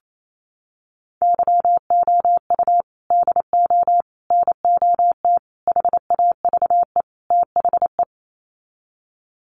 Morse code, heard as a keyed tone, about 28 wpm, 700 Hz. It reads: YOU DO NOT HAVE THE
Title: Sets of 5 Words 28wpm